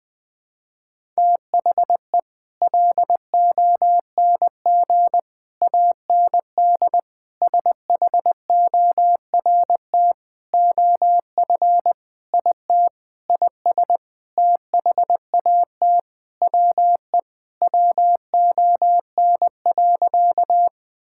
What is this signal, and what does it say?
Morse code, heard as a keyed tone, about 20 wpm, 700 Hz. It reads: THE LONG AND SHORT OF IT IS THAT WE WON.
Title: Sentences with 8-Word Phrases - 20wpm